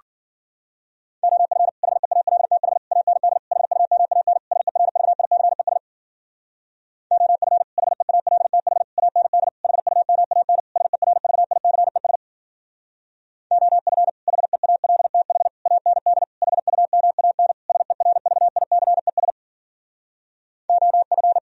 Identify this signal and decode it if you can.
Morse code, heard as a keyed tone, about 51 wpm, 700 Hz. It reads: OF HEALTH AND HUMAN SERVICES OF HEALTH AND HUMAN SERVICES OF HEALTH AND HUMAN SERVICES OF